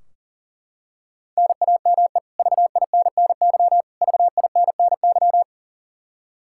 Morse code, heard as a keyed tone, about 40 wpm, 700 Hz. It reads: NAME VINNY VINNY
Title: Practice QSO - Exchange 1 - Beginning - 40wpm